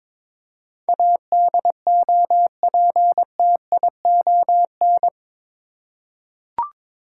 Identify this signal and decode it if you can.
Morse code, heard as a keyed tone, about 21 wpm, 700 Hz. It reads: ADOPTION
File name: practice / 22wpm / Eight Letter Words - 22wpm